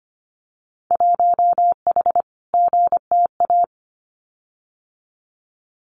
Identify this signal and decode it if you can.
Morse code, heard as a keyed tone, about 25 wpm, 700 Hz. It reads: 1H GTA
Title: Winter Field Day Exchanges Encore 25wpm